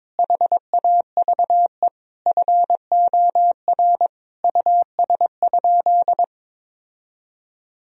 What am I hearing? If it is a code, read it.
Morse code, heard as a keyed tone, about 22 wpm, 700 Hz. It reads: HAVE FOR US?
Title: Sentences from Top 1000 Words - Review Part 3 - 22wpm